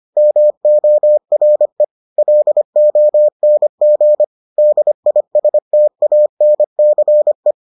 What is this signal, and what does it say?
Morse code, heard as a keyed tone, about 25 wpm, 600 Hz. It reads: MORE LONG DISTANCE